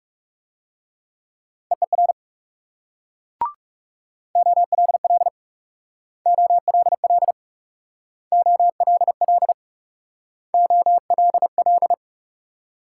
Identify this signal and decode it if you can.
Morse code, heard as a keyed tone, about 38 wpm, 700 Hz. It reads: EER OLL OLL OLL OLL